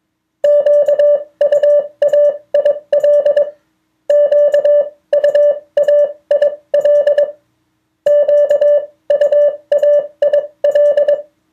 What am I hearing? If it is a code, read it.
Morse code, heard as a keyed tone, about 22 wpm, spaced as for 18 wpm, 580 Hz. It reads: QUAIL QUAIL QUAIL
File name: 2of4-words-1